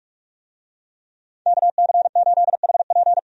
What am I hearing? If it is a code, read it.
Morse code, heard as a keyed tone, about 46 wpm, 700 Hz. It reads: KC8HP
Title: USA Call Signs - 5 Characters - 45wpm